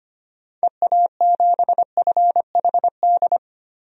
Morse code, heard as a keyed tone, about 25 wpm, 700 Hz. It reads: EA7FHD
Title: DXCC-100 Call Signs - 6 Characters - 25wpm